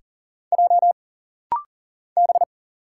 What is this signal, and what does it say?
Morse code, heard as a keyed tone, about 40 wpm, 700 Hz. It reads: J B
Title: Single Letters - Rapid-Fire 40wpm